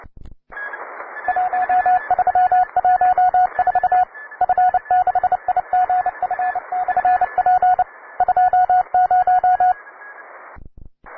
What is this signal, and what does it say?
Morse code, heard as a keyed tone, about 29 wpm, 730 Hz, with radio noise. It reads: 1314 F6IGF/P 20
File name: wae / cw advanced / rk9jwv cw